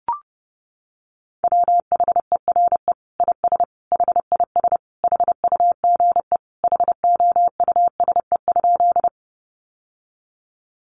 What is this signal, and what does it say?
Morse code, heard as a keyed tone, about 28 wpm, 700 Hz. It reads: WHERE IS HIS HUGE HOUSE?
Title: Sentences from Top 1100 Words 30wpm